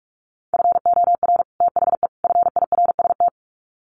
Morse code, heard as a keyed tone, about 44 wpm, 700 Hz. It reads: FOR THE FIRST
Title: Sets of 3 Words 45wpm